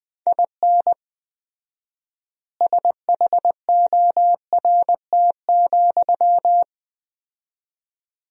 Morse code, heard as a keyed tone, about 20 wpm, 700 Hz. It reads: IN SHORT,
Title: 4x Word-Spacing - Sentences from Top 400 Words - 20wpm